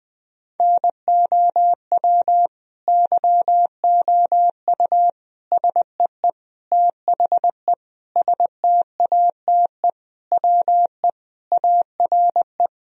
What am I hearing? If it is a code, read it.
Morse code, heard as a keyed tone, about 21 wpm, 700 Hz. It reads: NOW YOU SEE THE STATE WE ARE
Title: Sentences from Top 1000 Words - Review Part 2 - 20wpm